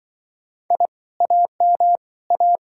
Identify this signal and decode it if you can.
Morse code, heard as a keyed tone, about 24 wpm, 700 Hz. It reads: I AM A
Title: Speed-Racing Common QSO Phrases Encore - 20wpm